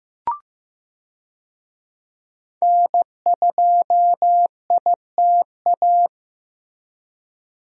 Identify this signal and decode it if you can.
Morse code, heard as a keyed tone, about 15 wpm, 700 Hz. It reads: N2ITA
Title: USA Call Signs - 5 Characters - Repeated 3x - 15wpm